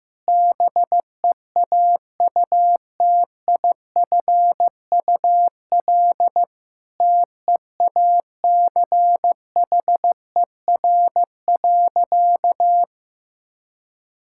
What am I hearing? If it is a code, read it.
Morse code, heard as a keyed tone, about 15 wpm, 700 Hz. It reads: BEAUTIFUL TEACHER.
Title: Sentences from Top 1000 Words - Review Part 6 - 15wpm